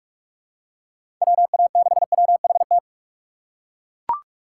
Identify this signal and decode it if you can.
Morse code, heard as a keyed tone, about 42 wpm, 700 Hz. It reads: WA6WHT
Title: USA Call Signs - 6 Characters - 2x Character Spacing - 45wpm